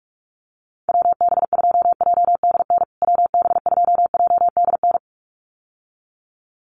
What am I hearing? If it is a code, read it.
Morse code, heard as a keyed tone, about 44 wpm, 700 Hz. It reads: WB2JDN WB2JDN
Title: USA Call Signs Repeated 2x 45wpm